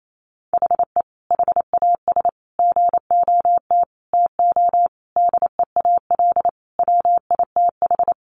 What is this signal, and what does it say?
Morse code, heard as a keyed tone, about 28 wpm, 700 Hz. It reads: HE HAS GOT TO DEAL WITH
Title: Sentences from Top 800 Words 28wpm